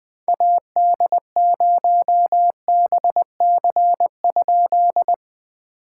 Morse code, heard as a keyed tone, about 21 wpm, 700 Hz. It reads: AD0BC?
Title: POTA QSO Practice - 20wpm